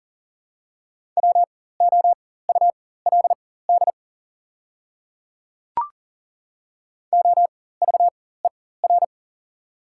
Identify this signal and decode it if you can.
Morse code, heard as a keyed tone, about 40 wpm, spaced as for 19 wpm, 690 Hz. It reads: WOULD OVER
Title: 4x Character Spacing - Top 100 Words - 40wpm